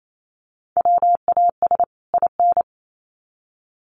Morse code, heard as a keyed tone, about 28 wpm, 700 Hz. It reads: WAS IN
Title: Sets of 2 Words 28wpm